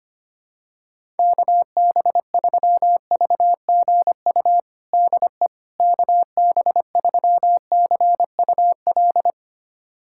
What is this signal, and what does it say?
Morse code, heard as a keyed tone, about 25 wpm, 700 Hz. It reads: KB3VGU DE KB3CUL
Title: USA Callsign DE Callsign 25wpm